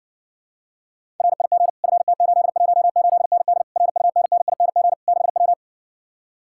Morse code, heard as a keyed tone, about 60 wpm, 700 Hz. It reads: RIG FT818ND RUNNING 6W